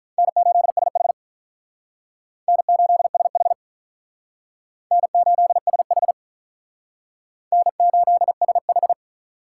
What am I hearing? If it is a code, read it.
Morse code, heard as a keyed tone, about 45 wpm, 700 Hz. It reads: N8SH N8SH N8SH N8SH